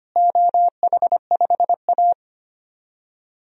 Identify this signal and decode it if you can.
Morse code, heard as a keyed tone, about 25 wpm, 700 Hz. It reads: OH5A
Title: DXCC-100 Call Signs - 4 Characters - Repeated 3x - 25wpm